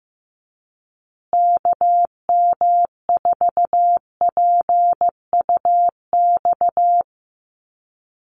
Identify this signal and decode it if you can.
Morse code, heard as a keyed tone, about 14 wpm, 700 Hz. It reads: KM4PUX
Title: USA Call Signs Repeated 3x 15wpm